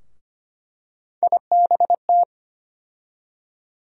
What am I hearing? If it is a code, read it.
Morse code, heard as a keyed tone, about 25 wpm, 700 Hz. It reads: IBT
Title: USA Call Sign Postfixes - 3 Letters 25wpm